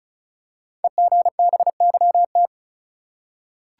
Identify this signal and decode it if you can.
Morse code, heard as a keyed tone, about 36 wpm, 700 Hz. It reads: EGBYT
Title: Code Groups - 5 Letter:Numbers - 35wpm